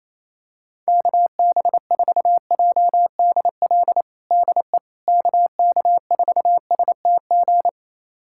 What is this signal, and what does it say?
Morse code, heard as a keyed tone, about 28 wpm, 700 Hz. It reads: KB4JDL DE KK4STG
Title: USA Callsign DE Callsign 28wpm